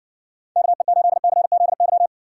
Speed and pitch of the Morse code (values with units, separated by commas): 60 wpm, 700 Hz